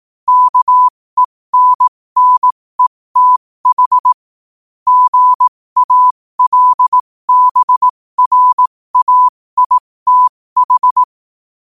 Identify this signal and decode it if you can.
Morse code, heard as a keyed tone, about 18 wpm, spaced as for 15 wpm, 1000 Hz. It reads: KENNETH GALBRAITH
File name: 15wpm